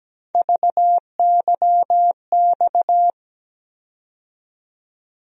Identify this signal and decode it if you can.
Morse code, heard as a keyed tone, about 17 wpm, 700 Hz. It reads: VYX
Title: USA Call Sign Postfixes - 3 Letters 17wpm